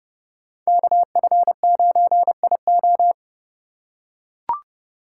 Morse code, heard as a keyed tone, about 28 wpm, 700 Hz. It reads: KF9IO